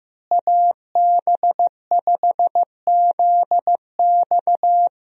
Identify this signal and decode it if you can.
Morse code, heard as a keyed tone, about 16 wpm, 700 Hz. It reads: AB5ZX
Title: USA Call Signs - 5 Characters - Repeated 3x - 15wpm